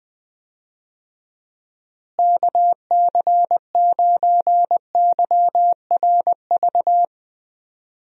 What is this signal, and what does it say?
Morse code, heard as a keyed tone, about 20 wpm, 700 Hz. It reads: KC9YRV
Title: USA Call Signs - 6 Characters - Repeated 2x - 20wpm